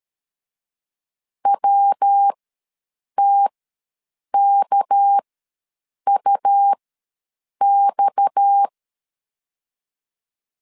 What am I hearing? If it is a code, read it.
Morse code, heard as a keyed tone, about 13 wpm, spaced as for 7 wpm, 800 Hz. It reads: WTKUX
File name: LEZIONI / 6 PKX0 / 6 PKX ws 12wpm 3106